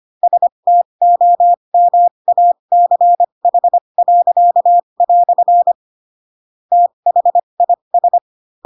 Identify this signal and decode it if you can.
Morse code, heard as a keyed tone, about 25 wpm, spaced as for 22 wpm, 700 Hz. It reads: STOMACH." THIS